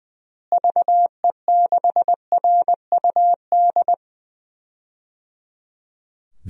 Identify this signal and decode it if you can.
Morse code, heard as a keyed tone, about 19 wpm, 700 Hz. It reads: VE6RUD
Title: DXCC-100 Call Signs - 6 Characters - Repeated 3x - 20wpm